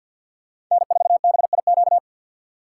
Morse code, heard as a keyed tone, about 48 wpm, 700 Hz. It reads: N4BIQ
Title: USA Call Signs - 5 Characters - 6x Character Spacing - 50wpm